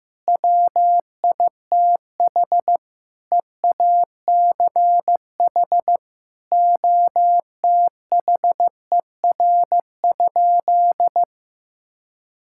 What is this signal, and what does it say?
Morse code, heard as a keyed tone, about 15 wpm, 700 Hz. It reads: WITH EACH OTHER?
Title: Sentences from Top 1000 Words - Review Part 6 - 15wpm